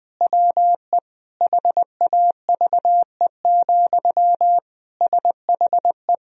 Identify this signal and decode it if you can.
Morse code, heard as a keyed tone, about 20 wpm, 700 Hz. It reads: WE HAVE, SHE
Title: Sentences from Top 1000 Words - Review Part 4 - 20wpm